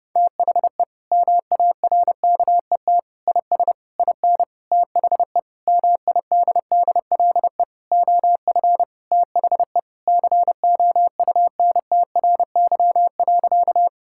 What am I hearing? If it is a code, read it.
Morse code, heard as a keyed tone, about 28 wpm, 700 Hz. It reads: THE MARKET IS IN THE MIDDLE OF THE COUNTRY.
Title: Sentences from Top 1000 Words - Review Part 6 - 30wpm